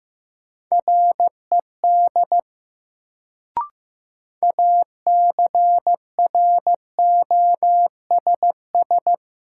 Morse code, heard as a keyed tone, about 15 wpm, 700 Hz. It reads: RED ACROSS
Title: Sentences from Top 700 Words - New Word Drill - 15wpm